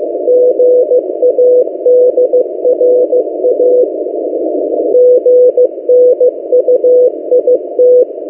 Morse code, heard as a keyed tone, about 15 wpm, 500 Hz. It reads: GADRA GNUIT
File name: M08a- -1355utc-8096khz